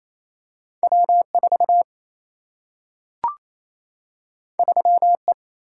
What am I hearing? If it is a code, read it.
Morse code, heard as a keyed tone, about 28 wpm, 700 Hz. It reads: W 4 3 E
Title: Single-Letter Single-Number 28wpm